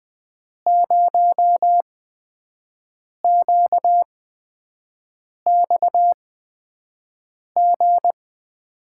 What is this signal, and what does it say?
Morse code, heard as a keyed tone, about 20 wpm, spaced as for 5 wpm, 700 Hz. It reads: 0QXG
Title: USA Call Signs - 5 Characters - 8x Character Spacing - 20wpm